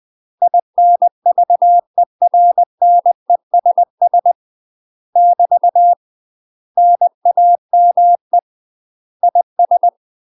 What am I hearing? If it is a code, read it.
Morse code, heard as a keyed tone, about 20 wpm, 700 Hz. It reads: INVERNESS <BT> NAME IS
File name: UK MASTER QSOs TRAINING / 20WPM / UK MASTER QSOs 0041